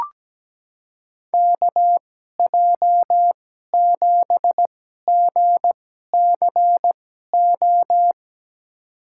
Morse code, heard as a keyed tone, about 17 wpm, spaced as for 12 wpm, 700 Hz. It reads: KJ7GCO
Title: USA Call Signs - 2x Character Spacing - 17wpm